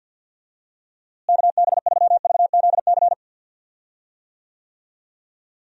Morse code, heard as a keyed tone, about 48 wpm, 700 Hz. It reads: KB3VZC
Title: USA Call Signs - 6 Characters - Repeated 3x - 50wpm